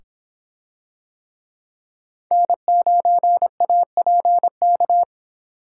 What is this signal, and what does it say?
Morse code, heard as a keyed tone, about 25 wpm, 700 Hz. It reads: N9APK